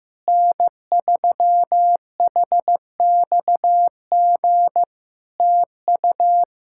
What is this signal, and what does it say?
Morse code, heard as a keyed tone, about 15 wpm, 700 Hz. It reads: N3HXG TU
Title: K1USN SST QSO Practice 15wpm